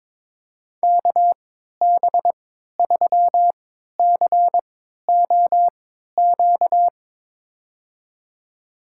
Morse code, heard as a keyed tone, about 21 wpm, spaced as for 12 wpm, 700 Hz. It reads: KB3COQ
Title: USA Call Signs - 6 Characters - 3x Character Spacing - 22wpm